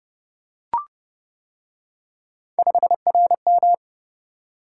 Morse code, heard as a keyed tone, about 28 wpm, 700 Hz. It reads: 5RM